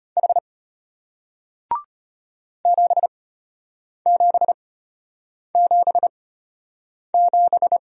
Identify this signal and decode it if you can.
Morse code, heard as a keyed tone, about 34 wpm, spaced as for 8 wpm, 700 Hz. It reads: H 7777